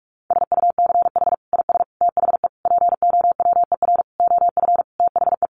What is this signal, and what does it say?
Morse code, heard as a keyed tone, about 44 wpm, 700 Hz. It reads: SUCH IS THE POWER OF THE
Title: Sentences from Top 1400 Words 45wpm